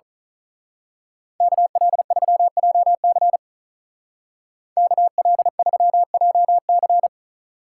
Morse code, heard as a keyed tone, about 39 wpm, 700 Hz. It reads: KL3JC KL3JC
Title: Speed Racing - USA 2x2 Call Signs 35wpm